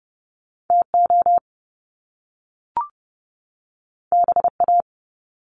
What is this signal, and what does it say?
Morse code, heard as a keyed tone, about 30 wpm, 700 Hz. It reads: TO BA
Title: Top Two Letter Combinations in Words 30wpm